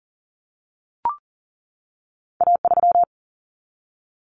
Morse code, heard as a keyed tone, about 40 wpm, 700 Hz. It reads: A3
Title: Common Cut Numbers - 2-Digit 40wpm